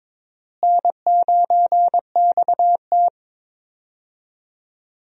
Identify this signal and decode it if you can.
Morse code, heard as a keyed tone, about 22 wpm, 700 Hz. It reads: N9XT
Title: USA Call Signs - 4 Characters - Repeated 3x - 22wpm